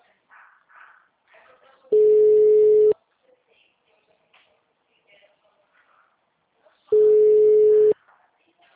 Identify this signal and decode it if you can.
Ringback tone (Europe, 425 Hz), with speech.